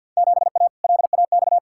Morse code, heard as a keyed tone, about 48 wpm, 700 Hz. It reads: 7A LAX
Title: Field Day Exchanges - 50wpm